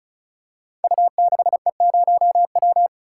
Morse code, heard as a keyed tone, about 36 wpm, 700 Hz. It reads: U6E0W
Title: Code Groups - 5 Letter:Numbers - 35wpm